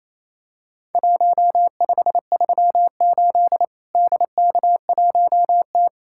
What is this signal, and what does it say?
Morse code, heard as a keyed tone, about 28 wpm, 700 Hz. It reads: 1538 DK1T
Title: Worked All Europe DX Contest - QTC Exchange - 28wpm